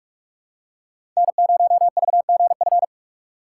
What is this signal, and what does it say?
Morse code, heard as a keyed tone, about 47 wpm, 700 Hz. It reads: N0VGF
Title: USA Call Signs - 2x Character Spacing - 45wpm